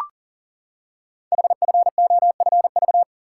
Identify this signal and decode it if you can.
Morse code, heard as a keyed tone, about 40 wpm, 690 Hz. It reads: HFOFV
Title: Code Groups - 5 Letters - 40wpm